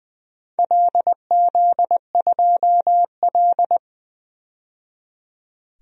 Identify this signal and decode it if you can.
Morse code, heard as a keyed tone, about 20 wpm, 700 Hz. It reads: LZ2L
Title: DXCC-100 Call Signs - 4 Characters - 20wpm